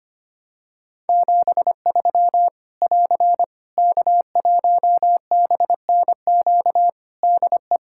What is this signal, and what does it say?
Morse code, heard as a keyed tone, about 25 wpm, 700 Hz. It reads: 73 <AR> K1BNQ DE
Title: Practice QSO - Exchange 4 - Ending - 25wpm